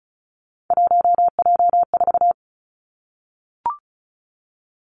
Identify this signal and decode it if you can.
Morse code, heard as a keyed tone, about 36 wpm, 700 Hz. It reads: 1J4
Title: Three Challenging Letter-Number Combinations 35wpm